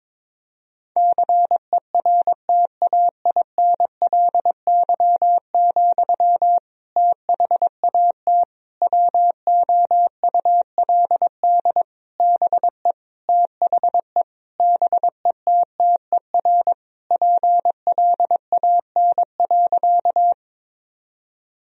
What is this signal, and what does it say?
Morse code, heard as a keyed tone, about 22 wpm, 700 Hz. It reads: CERTAINLY, THAT WOULD BE THE BETTER PLAN.
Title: Sentences from Top 1000 Words - Review Part 2 - 22wpm